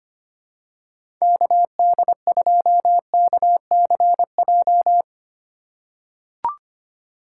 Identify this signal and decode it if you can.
Morse code, heard as a keyed tone, about 25 wpm, 700 Hz. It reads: KD2KCJ
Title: USA Call Signs - 6 Characters - 2x Character Spacing - 25wpm